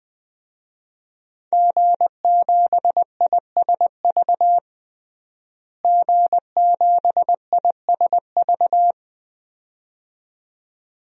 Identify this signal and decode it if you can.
Morse code, heard as a keyed tone, about 21 wpm, 700 Hz. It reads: G7ISV G7ISV
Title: DXCC-100 Call Signs - 5 Characters - Repeated 2x - 20wpm